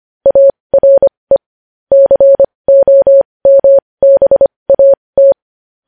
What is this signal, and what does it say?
Morse code, heard as a keyed tone, about 25 wpm, spaced as for 20 wpm, 550 Hz. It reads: ARE COMBAT
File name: podcast / News headlines-1x-25x20